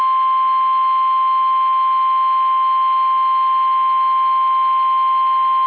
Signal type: Emergency Alert System (SAME) tones